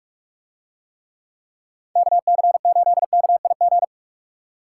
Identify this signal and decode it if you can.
Morse code, heard as a keyed tone, about 45 wpm, 700 Hz. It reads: KC8KIG